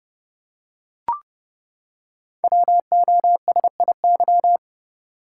Morse code, heard as a keyed tone, about 29 wpm, 700 Hz. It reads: WOSIY
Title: Code Groups - 5 Letter:Numbers - 30wpm